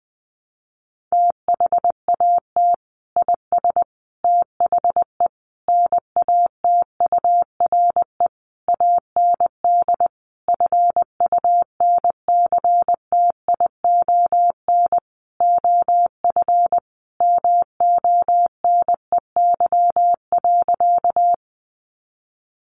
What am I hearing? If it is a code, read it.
Morse code, heard as a keyed tone, about 20 wpm, 700 Hz. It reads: THAT IS THE NATURE AND FUNCTION OF MONEY.
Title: Sentences from Top 1100 Words 20wpm